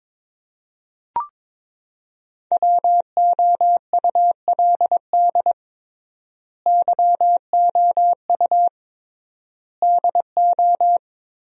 Morse code, heard as a keyed tone, about 21 wpm, 700 Hz. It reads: WOULD YOU DO
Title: 3x Word-Spacing - Sentences from Top 100 Words - 22wpm